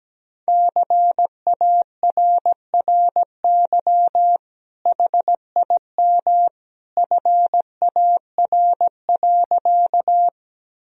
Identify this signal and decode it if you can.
Morse code, heard as a keyed tone, about 17 wpm, 700 Hz. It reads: CARRY HIM FAR.
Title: Sentences from Top 1000 Words - Review Part 1 - 17wpm